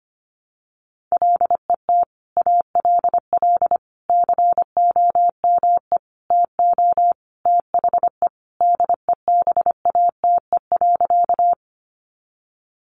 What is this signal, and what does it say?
Morse code, heard as a keyed tone, about 25 wpm, 700 Hz. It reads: LET ALL COME TO THE DEBATE.
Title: Sentences from Top 1100 Words 25wpm